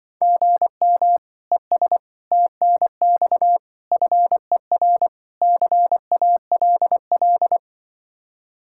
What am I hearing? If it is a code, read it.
Morse code, heard as a keyed tone, about 24 wpm, 700 Hz. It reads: GM ES TNX FER CALL
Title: Speed-Racing Common QSO Phrases Encore - 20wpm